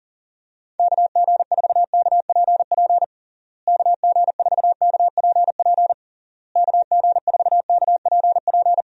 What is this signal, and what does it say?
Morse code, heard as a keyed tone, about 40 wpm, 700 Hz. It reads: KG4KPP KG4KPP KG4KPP
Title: USA Call Signs - 6 Characters - Repeated 3x - 40wpm